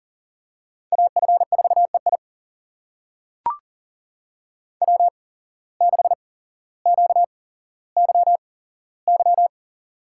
Morse code, heard as a keyed tone, about 39 wpm, spaced as for 17 wpm, 700 Hz. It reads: AF4EI W6QYY
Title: USA Call Signs - 5 Characters - 8x Character Spacing - 40wpm